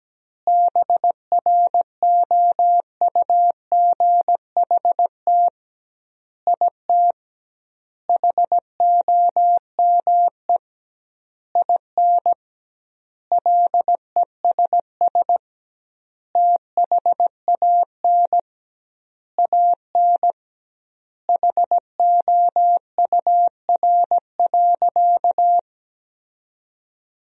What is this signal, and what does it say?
Morse code, heard as a keyed tone, about 17 wpm, 700 Hz. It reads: BROUGHT IT HOME IN LESS THAN AN HOUR.
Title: 2x Word-Spacing - Sentences from Top 500 Words - 17wpm